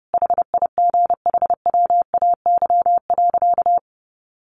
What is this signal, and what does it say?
Morse code, heard as a keyed tone, about 28 wpm, 700 Hz. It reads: HIGHWAY.
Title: Sentences from Top 1400 Words 30wpm